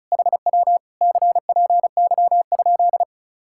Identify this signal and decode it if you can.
Morse code, heard as a keyed tone, about 37 wpm, 700 Hz. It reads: HW CPY?
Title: Speed-Racing Common QSO Phrases Encore - 30wpm